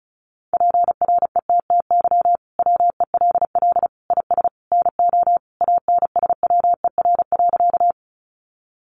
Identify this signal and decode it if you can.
Morse code, heard as a keyed tone, about 34 wpm, 700 Hz. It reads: PRETTY WELL IS NO ANSWER.
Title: Sentences from Top 1000 Words 35wpm